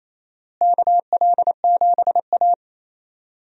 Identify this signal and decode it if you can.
Morse code, heard as a keyed tone, about 28 wpm, 700 Hz. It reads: KL7A
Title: DXCC-100 Call Signs - 4 Characters - Repeated 3x - 28wpm